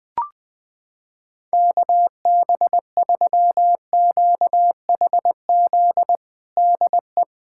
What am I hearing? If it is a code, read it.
Morse code, heard as a keyed tone, about 20 wpm, 700 Hz. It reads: KB3QHZ DE
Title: USA Callsign DE Callsign - 6 Characters - 20wpm